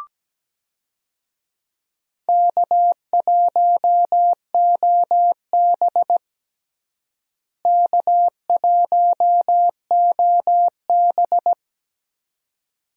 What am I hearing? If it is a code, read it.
Morse code, heard as a keyed tone, about 17 wpm, 700 Hz. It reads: K1OB K1OB
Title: USA Call Signs - 4 Characters - Repeated 2x - 17wpm